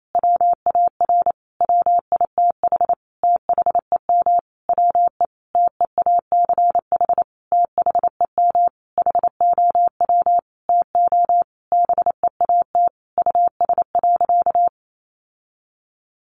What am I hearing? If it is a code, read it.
Morse code, heard as a keyed tone, about 28 wpm, 700 Hz. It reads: WAR WITH THEM WE TEACH THEM HOW TO BEAT US.
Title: Sentences from Top 1300 Words 28wpm